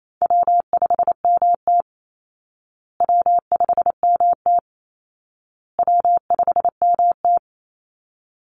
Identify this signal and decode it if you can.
Morse code, heard as a keyed tone, about 28 wpm, 700 Hz. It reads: W5MT W5MT W5MT
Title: Easy Call Signs Repeated 3x - 28wpm